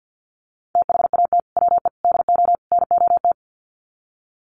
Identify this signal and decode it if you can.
Morse code, heard as a keyed tone, about 50 wpm, 700 Hz. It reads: THAT WE DO NOT
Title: Sets of 4 Words 50wpm